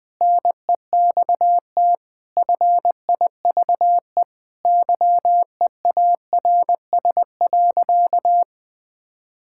Morse code, heard as a keyed tone, about 20 wpm, 700 Hz. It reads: NEXT FIVE YEARS.
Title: Sentences with 5-Word Phrases - 20wpm